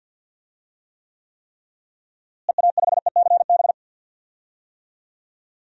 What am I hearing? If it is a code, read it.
Morse code, heard as a keyed tone, about 51 wpm, 700 Hz. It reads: EA5ECB
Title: DXCC-100 Call Signs - 6 Characters - Repeated 3x - 50wpm